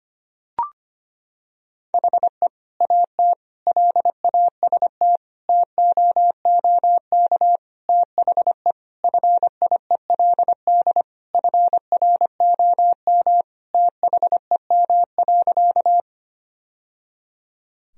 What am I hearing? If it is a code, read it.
Morse code, heard as a keyed tone, about 25 wpm, 700 Hz. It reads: HE AT LAST TOOK THE FIELD FROM THEM.
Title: Sentences from Top 1000 Words - Review Part 2 - 25wpm